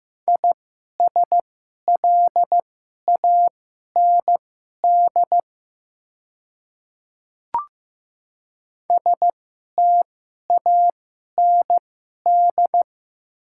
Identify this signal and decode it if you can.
Morse code, heard as a keyed tone, about 15 wpm, spaced as for 11 wpm, 690 Hz. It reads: ISLAND STAND
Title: 2x Character Spacing - Top 500 Words - 15wpm